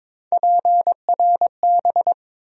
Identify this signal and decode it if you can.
Morse code, heard as a keyed tone, about 22 wpm, 700 Hz. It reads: PRB
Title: USA Call Sign Postfixes - 3 Letters 22wpm